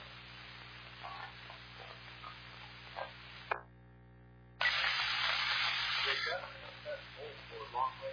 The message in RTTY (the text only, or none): none